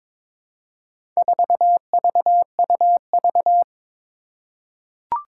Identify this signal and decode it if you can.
Morse code, heard as a keyed tone, about 21 wpm, 700 Hz. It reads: 4VUV
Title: Four Challenging Letter-Number Combinations 22wpm